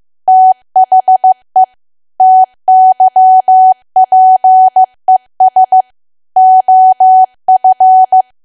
Morse code, heard as a keyed tone, about 15 wpm, 750 Hz. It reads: THE TYPES OF
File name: cwa / W1AW-Level2 / 15WPM